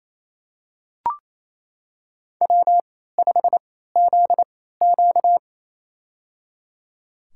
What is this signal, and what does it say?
Morse code, heard as a keyed tone, about 28 wpm, spaced as for 16 wpm, 700 Hz. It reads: W5ZQ
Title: USA Call Signs - 4 Characters - 3x Character Spacing - 28wpm